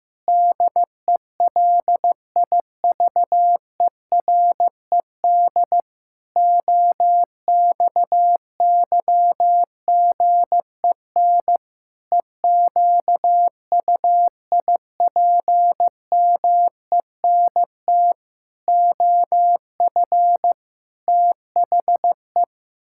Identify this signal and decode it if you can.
Morse code, heard as a keyed tone, about 15 wpm, 700 Hz. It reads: DELIVERED OXYGEN EQUIPMENT OF THE
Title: Pangram Challenge v2 - 15wpm